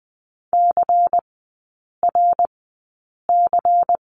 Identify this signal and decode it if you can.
Morse code, heard as a keyed tone, about 20 wpm, spaced as for 8 wpm, 700 Hz. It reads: CRC
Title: 100 Lesson 17e - Letter u - Three Characters 20wpm